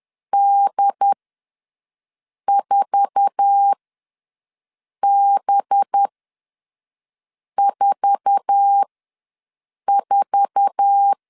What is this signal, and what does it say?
Morse code, heard as a keyed tone, about 10 wpm, spaced as for 5 wpm, 800 Hz. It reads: D4B44
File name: LEZIONI / 8 YJ45 / 8 YJ4 ws 10wpm 4020